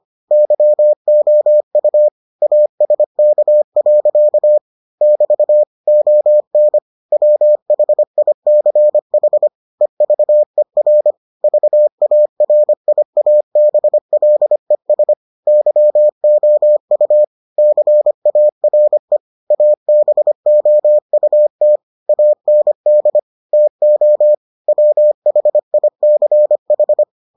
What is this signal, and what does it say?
Morse code, heard as a keyed tone, about 25 wpm, 600 Hz. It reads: YOU ASK. <BT> ON WHICH EVER VARIABLES YOU CARE ABOUT AND TO WHICH